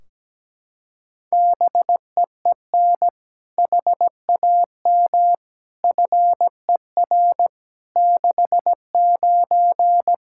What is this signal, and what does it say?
Morse code, heard as a keyed tone, about 17 wpm, 700 Hz. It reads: BEEN HAM FER 69
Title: Practice QSO - Exchange 3 - More Info - 17wpm